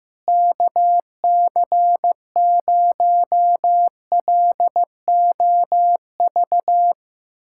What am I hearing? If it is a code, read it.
Morse code, heard as a keyed tone, about 15 wpm, 700 Hz. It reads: KC0LOV
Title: SOTA QSO Practice 15wpm